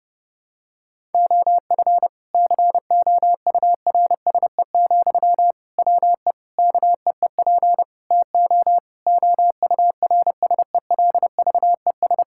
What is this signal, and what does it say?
Morse code, heard as a keyed tone, about 30 wpm, 700 Hz. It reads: OF COURSE, WE KEEP TO OURSELVES
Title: Sentences from Top 1600 Words 30wpm